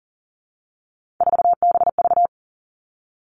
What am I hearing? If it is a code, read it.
Morse code, heard as a keyed tone, about 37 wpm, 700 Hz. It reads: 4BV
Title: Three Challenging Letter-Number Combinations 40wpm